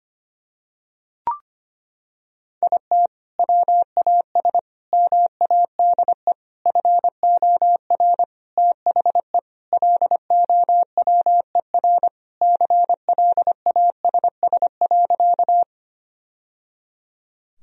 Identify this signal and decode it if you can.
Morse code, heard as a keyed tone, about 25 wpm, 700 Hz. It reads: IT WAS MADE FOR THE LOWER CLASS.
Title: Sentences from Top 1000 Words - Review Part 4 - 25wpm